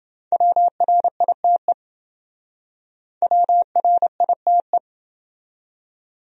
Morse code, heard as a keyed tone, about 28 wpm, 700 Hz. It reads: WRITE WRITE